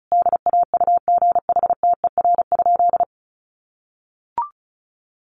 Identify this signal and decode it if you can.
Morse code, heard as a keyed tone, about 34 wpm, 700 Hz. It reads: DAUGHTER?
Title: Sentences from Top 1500 Words 35wpm